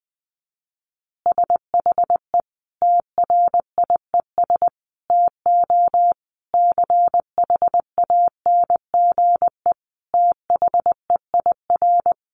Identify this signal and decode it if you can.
Morse code, heard as a keyed tone, about 20 wpm, 700 Hz. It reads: SHE TRIES TO CHANGE THEIR
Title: Easy Phrases 20wpm